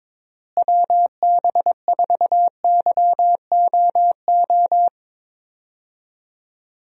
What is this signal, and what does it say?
Morse code, heard as a keyed tone, about 22 wpm, 700 Hz. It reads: WB4YOO
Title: USA Call Signs - 6 Characters - Repeated 3x - 22wpm